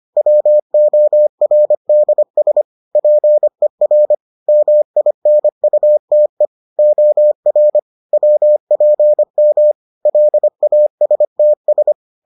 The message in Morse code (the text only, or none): WORDS PER MINUTE OR WPM LASTS